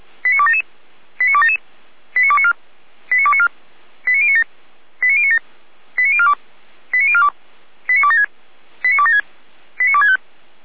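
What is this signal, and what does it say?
Five-tone selective call (ZVEI), heard as quick runs of five tones, tones of about 70 ms. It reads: 89277 89277 89294 89294 89097 89097 89042 89042 89276 89276 89265
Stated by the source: ZVEI five-tone selective call received by SDR, code 89310.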